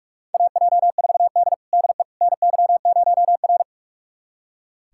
Morse code, heard as a keyed tone, about 46 wpm, 700 Hz. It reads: AJ4D DE NY0R
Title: USA Callsign DE Callsign - 4 Characters - 45wpm